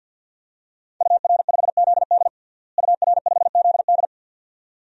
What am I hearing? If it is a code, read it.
Morse code, heard as a keyed tone, about 51 wpm, 700 Hz. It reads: UR5ZD UR5ZD
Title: DXCC-100 Call Signs - 5 Characters - Repeated 2x - 50wpm